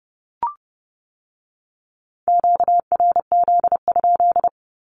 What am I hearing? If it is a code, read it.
Morse code, heard as a keyed tone, about 30 wpm, 690 Hz. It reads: QRZ?
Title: Q-Codes with Spoken Meaning - 30wpm